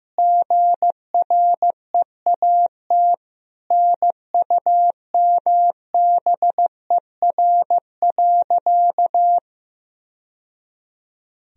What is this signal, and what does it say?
Morse code, heard as a keyed tone, about 16 wpm, 700 Hz. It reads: GREAT NUMBER.
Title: Sentences from Top 1000 Words - Review Part 1 - 15wpm